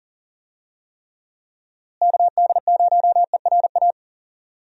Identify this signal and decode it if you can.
Morse code, heard as a keyed tone, about 40 wpm, 700 Hz. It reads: KD0ERA
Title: USA Call Signs - 6 Characters - Repeated 2x - 40wpm